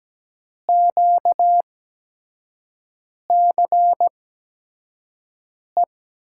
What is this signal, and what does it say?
Morse code, heard as a keyed tone, about 17 wpm, spaced as for 5 wpm, 700 Hz. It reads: QCE
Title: USA Call Signs - 6 Characters - 8x Character Spacing - 17wpm